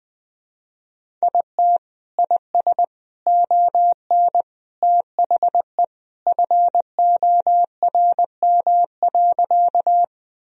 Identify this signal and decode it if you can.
Morse code, heard as a keyed tone, about 21 wpm, 700 Hz. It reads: IT IS ON THE FORM.
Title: Sentences from Top 1000 Words - Review Part 1 - 20wpm